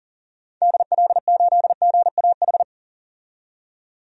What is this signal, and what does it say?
Morse code, heard as a keyed tone, about 40 wpm, 700 Hz. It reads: DL8GAH
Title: DXCC-100 Call Signs - 6 Characters - 40wpm